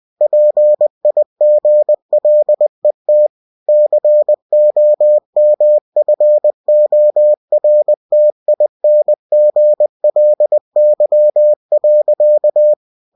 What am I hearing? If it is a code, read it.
Morse code, heard as a keyed tone, about 20 wpm, 600 Hz. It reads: PIGLET COMFORTINGLY.
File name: cw-books / winnie-the-pooh / winnie-20wpm-page-0063